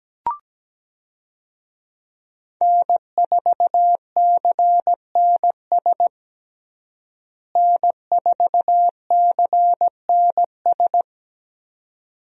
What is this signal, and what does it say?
Morse code, heard as a keyed tone, about 17 wpm, 700 Hz. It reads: N4CNS N4CNS
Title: USA Call Signs - 5 Characters - Repeated 2x - 17wpm